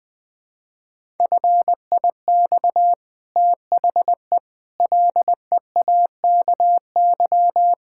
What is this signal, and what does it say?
Morse code, heard as a keyed tone, about 20 wpm, 700 Hz. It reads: FIX THE LEAKY
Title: Easy Sentences with Hints - 20wpm